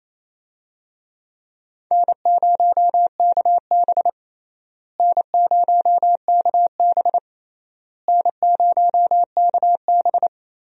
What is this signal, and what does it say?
Morse code, heard as a keyed tone, about 28 wpm, 700 Hz. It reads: N0KB N0KB N0KB
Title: USA Call Signs - 4 Characters - Repeated 3x - 28wpm